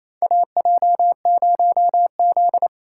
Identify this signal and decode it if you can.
Morse code, heard as a keyed tone, about 28 wpm, 700 Hz. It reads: AJ0Z
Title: USA Call Signs - 4 Characters - 2x Character Spacing - 28wpm